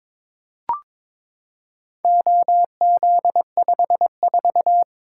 Morse code, heard as a keyed tone, about 22 wpm, 700 Hz. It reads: OZ54
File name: practice / 22wpm / Four Letter-Number Combinations - 22wpm